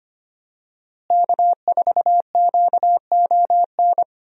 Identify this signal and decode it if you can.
Morse code, heard as a keyed tone, about 25 wpm, 700 Hz. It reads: K4QON
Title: USA Call Signs - 5 Characters - Repeated 3x - 25wpm